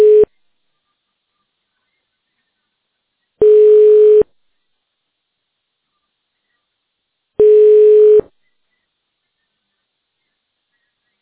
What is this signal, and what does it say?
Ringback tone (425 Hz, 1 s on, 3 s off).